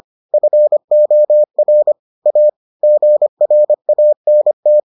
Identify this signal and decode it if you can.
Morse code, heard as a keyed tone, about 25 wpm, 600 Hz. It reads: FOR A GRANT